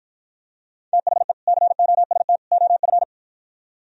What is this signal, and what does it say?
Morse code, heard as a keyed tone, about 52 wpm, 700 Hz. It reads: THE COST OF